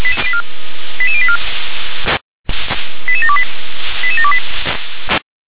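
Five-tone selective call (ZVEI), heard as quick runs of five tones, tones of about 70 ms. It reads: 88084 88728 88728
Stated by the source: ZVEI five-tone selective call received on an SDR.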